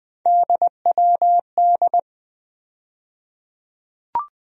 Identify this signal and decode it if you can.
Morse code, heard as a keyed tone, about 20 wpm, 700 Hz. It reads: DWD
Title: USA Call Sign Postfixes - 3 Letters 20wpm